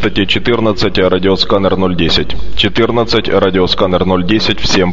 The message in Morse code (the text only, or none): none